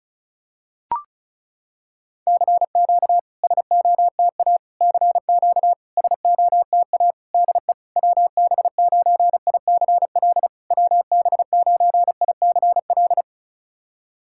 Morse code, heard as a keyed tone, about 36 wpm, 700 Hz. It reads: CQ SOTA CQ SOTA DE WB9ICL WB9ICL
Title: SOTA QSO Practice 35wpm